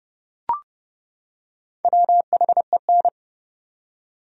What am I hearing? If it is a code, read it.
Morse code, heard as a keyed tone, about 30 wpm, 700 Hz. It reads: WHEN
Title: Top 100 Words 30wpm